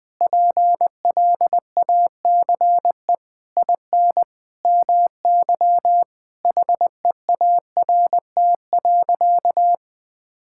Morse code, heard as a keyed tone, about 20 wpm, 700 Hz. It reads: PLACE IN MY HEART.
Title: Sentences with 9-Word Phrases - Spoken Beginning - 20wpm